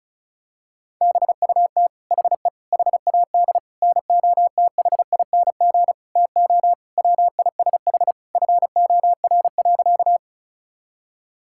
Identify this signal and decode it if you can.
Morse code, heard as a keyed tone, about 36 wpm, 700 Hz. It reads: BUT HE HAD NOTHING TO WISH FOR.
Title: Sentences from Top 1600 Words 35wpm